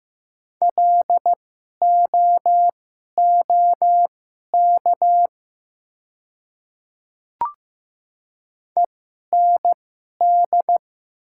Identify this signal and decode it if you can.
Morse code, heard as a keyed tone, about 15 wpm, spaced as for 11 wpm, 700 Hz. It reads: LOOK END
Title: 2x Character Spacing - Top 200 Words - 15wpm